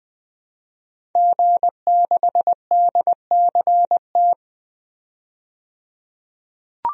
Morse code, heard as a keyed tone, about 20 wpm, 700 Hz. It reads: G6DCT
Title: DXCC-100 Call Signs - 5 Characters - 20wpm